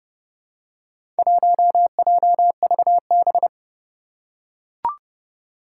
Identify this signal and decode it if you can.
Morse code, heard as a keyed tone, about 31 wpm, 700 Hz. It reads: 1JVB
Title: Four Challenging Letter-Number Combinations 30wpm